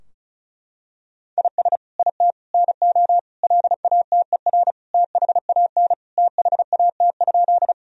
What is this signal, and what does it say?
Morse code, heard as a keyed tone, about 34 wpm, 700 Hz. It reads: IS IT NO LATER THAN THAT?
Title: Sentences from Top 1000 Words - Review Part 4 - 35wpm